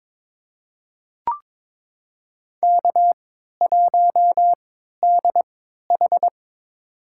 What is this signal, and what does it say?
Morse code, heard as a keyed tone, about 22 wpm, spaced as for 12 wpm, 700 Hz. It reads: K1DH
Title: USA Call Signs - 4 Characters - 3x Character Spacing - 22wpm